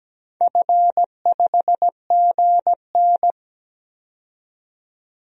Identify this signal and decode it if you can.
Morse code, heard as a keyed tone, about 17 wpm, 700 Hz. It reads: F5GN
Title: DXCC-100 Call Signs - 4 Characters - 17wpm